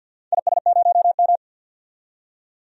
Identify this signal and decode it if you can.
Morse code, heard as a keyed tone, about 52 wpm, 700 Hz. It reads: IS0M
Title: DXCC-100 Call Signs - 4 Characters - 50wpm